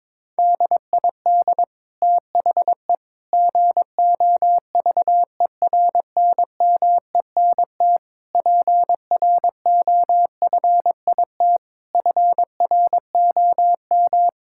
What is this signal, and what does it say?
Morse code, heard as a keyed tone, about 22 wpm, 700 Hz. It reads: DID THE GOVERNMENT PROFIT FROM
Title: Sentences from Top 1000 Words - Review Part 5 - 22wpm